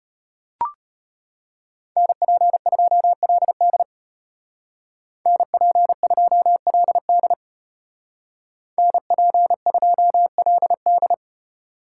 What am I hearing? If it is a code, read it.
Morse code, heard as a keyed tone, about 30 wpm, 700 Hz. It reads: NP2LD NP2LD NP2LD
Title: Speed Racing - USA Call Signs 25wpm